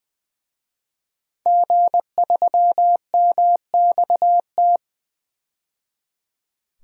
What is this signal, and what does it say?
Morse code, heard as a keyed tone, about 20 wpm, 700 Hz. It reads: G3MXT
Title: DXCC-100 Call Signs - 5 Characters - 20wpm